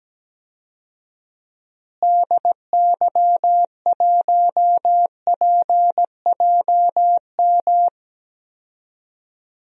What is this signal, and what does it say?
Morse code, heard as a keyed tone, about 17 wpm, 700 Hz. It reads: DY1PJM
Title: DXCC-100 Call Signs - 6 Characters - Repeated 3x - 17wpm